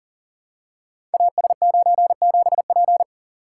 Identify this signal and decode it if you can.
Morse code, heard as a keyed tone, about 38 wpm, 700 Hz. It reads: AS97P
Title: Code Groups - 5 Letter:Numbers - 40wpm